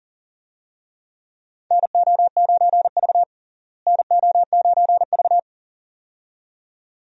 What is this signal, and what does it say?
Morse code, heard as a keyed tone, about 40 wpm, 700 Hz. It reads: NO9V NO9V
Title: USA Call Signs - 4 Characters - Repeated 2x - 40wpm